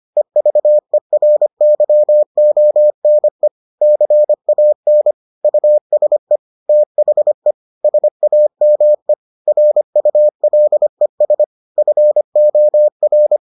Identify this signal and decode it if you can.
Morse code, heard as a keyed tone, about 25 wpm, 600 Hz. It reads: EVERYONE CAN USE THE SAME RULES FOR